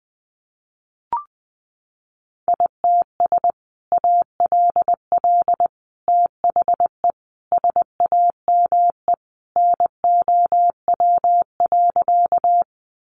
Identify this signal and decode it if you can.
Morse code, heard as a keyed tone, about 20 wpm, 700 Hz. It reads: ITS ALL THE SAME NOW.
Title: Sentences from Top 600 Words 20wpm